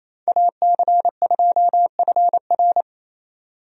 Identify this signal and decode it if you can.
Morse code, heard as a keyed tone, about 27 wpm, 700 Hz. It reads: AC2FR
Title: USA Call Signs - 5 Characters - Repeated 3x - 28wpm